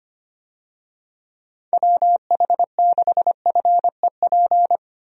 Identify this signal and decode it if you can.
Morse code, heard as a keyed tone, about 25 wpm, 700 Hz. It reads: WH6FEP
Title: USA Call Signs - 6 Characters - 25wpm